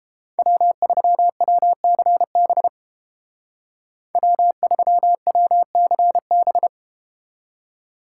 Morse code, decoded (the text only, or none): W3WCB W3WCB